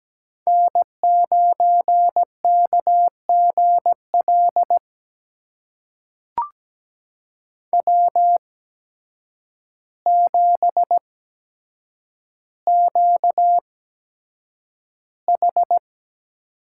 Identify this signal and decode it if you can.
Morse code, heard as a keyed tone, about 17 wpm, spaced as for 8 wpm, 700 Hz. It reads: N9KGL W7QH
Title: USA Call Signs - 8x Character Spacing - 17wpm